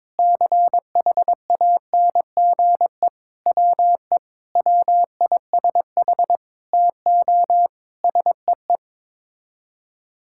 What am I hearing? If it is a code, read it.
Morse code, heard as a keyed tone, about 22 wpm, 700 Hz. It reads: CHANGE WE WISH TO SEE
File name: Sets of 9 Words - 22wpm